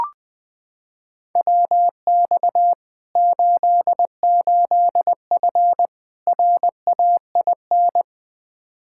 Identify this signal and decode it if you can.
Morse code, heard as a keyed tone, about 20 wpm, 700 Hz. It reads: WX 88F RAIN
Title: Practice QSO - Exchange 2 - Info - 20wpm